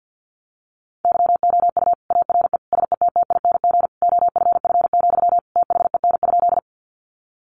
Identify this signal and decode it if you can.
Morse code, heard as a keyed tone, about 50 wpm, 700 Hz. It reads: YOU ARE SETTING OFF, THEN?
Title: Sentences from Top 1300 Words 50wpm